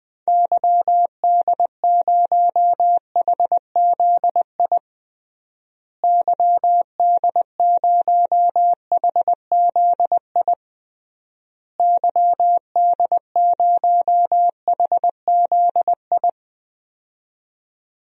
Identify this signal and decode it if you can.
Morse code, heard as a keyed tone, about 20 wpm, 700 Hz. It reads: YD0HZI YD0HZI YD0HZI
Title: DXCC-100 Call Signs - 6 Characters - Repeated 3x - 20wpm